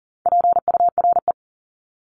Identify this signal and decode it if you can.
Morse code, heard as a keyed tone, about 40 wpm, 700 Hz. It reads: PURE
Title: Top 1251-1500 Words 40wpm